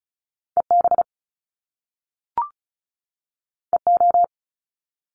Morse code, heard as a keyed tone, about 36 wpm, 700 Hz. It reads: EB EO
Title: Top Two Letter Combinations in Words 35wpm